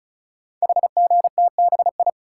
Morse code, heard as a keyed tone, about 36 wpm, 700 Hz. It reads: HGTBI
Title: Code Groups - 5 Letters - 35wpm